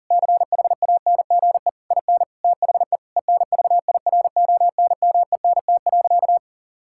Morse code, heard as a keyed tone, about 40 wpm, 700 Hz. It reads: CHANGE IN THE ENVIRONMENT.
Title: Sentences from Top 1000 Words - Review Part 6 - 40wpm